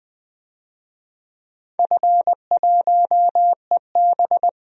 Morse code, heard as a keyed tone, about 20 wpm, 700 Hz. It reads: F1EB
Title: DXCC-100 Call Signs - 4 Characters - 20wpm